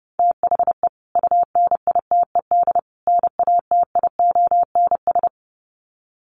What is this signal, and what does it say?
Morse code, heard as a keyed tone, about 30 wpm, 700 Hz. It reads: THE UNITED NATIONS
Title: Sets of 3 Words 30wpm